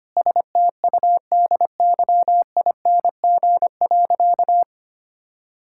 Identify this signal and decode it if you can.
Morse code, heard as a keyed tone, about 25 wpm, 700 Hz. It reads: STUDYING.
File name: practice / 25wpm / Difficult Sentences with Hints - 25wpm